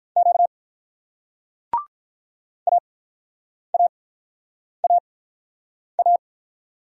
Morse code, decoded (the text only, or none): Q A A A A